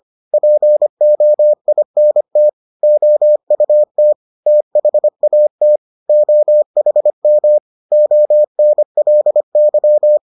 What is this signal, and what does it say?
Morse code, heard as a keyed tone, about 25 wpm, 600 Hz. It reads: POINT OUT THAT OHM ONLY